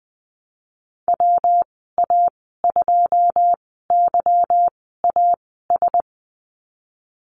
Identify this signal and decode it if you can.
Morse code, heard as a keyed tone, about 20 wpm, spaced as for 15 wpm, 700 Hz. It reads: WA2YAS
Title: 147 Lesson 24g - Letter y - Call Signs 20wpm